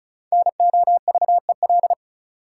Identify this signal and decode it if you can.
Morse code, heard as a keyed tone, about 34 wpm, 700 Hz. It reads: NOVEL
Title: Sentences from Top 1300 Words - New Word Drill Alt - 35wpm